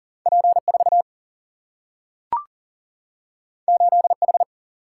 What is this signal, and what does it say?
Morse code, heard as a keyed tone, about 40 wpm, 700 Hz. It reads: P4 8H
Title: Single-Letter Single-Number 40wpm